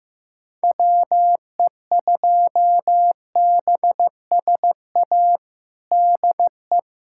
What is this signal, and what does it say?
Morse code, heard as a keyed tone, about 15 wpm, 700 Hz. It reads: WE2BSA DE
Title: USA Callsign DE Callsign - 6 Characters - 15wpm